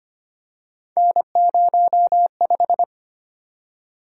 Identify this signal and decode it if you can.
Morse code, heard as a keyed tone, about 25 wpm, 700 Hz. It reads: N05
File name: Three Letter-Number Combinations - 25wpm